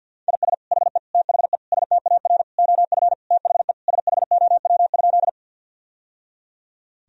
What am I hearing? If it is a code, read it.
Morse code, heard as a keyed tone, about 50 wpm, 700 Hz. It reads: IS HE THE STAR OF THE SHOW?
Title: Sentences from Top 1000 Words - Review Part 2 - 50wpm